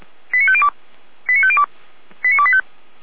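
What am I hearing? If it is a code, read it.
Five-tone selective call (ZVEI), heard as quick runs of five tones, tones of about 70 ms. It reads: 89502 89502 89285
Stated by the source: ZVEI five-tone selective call received on an SDR, from the Feuerwehr Orlamünde audio archive.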